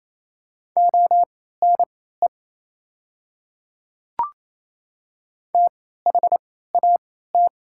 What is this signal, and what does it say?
Morse code, heard as a keyed tone, about 28 wpm, spaced as for 16 wpm, 700 Hz. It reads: ONE THAT
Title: 3x Character Spacing - Top 100 Words - 28wpm